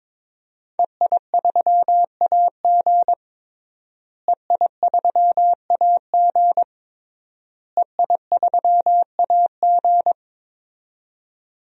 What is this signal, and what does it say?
Morse code, heard as a keyed tone, about 21 wpm, 700 Hz. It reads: EI3AG EI3AG EI3AG
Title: DXCC-100 Call Signs - 5 Characters - Repeated 3x - 22wpm